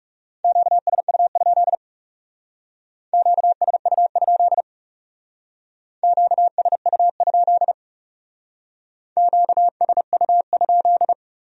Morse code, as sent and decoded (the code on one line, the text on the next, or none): --.- ... ..- ..--.. / --.- ... ..- ..--.. / --.- ... ..- ..--.. / --.- ... ..- ..--..
QSU? QSU? QSU? QSU?